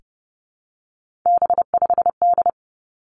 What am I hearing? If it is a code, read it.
Morse code, heard as a keyed tone, about 30 wpm, 700 Hz. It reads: B5D
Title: Three Challenging Letter-Number Combinations 30wpm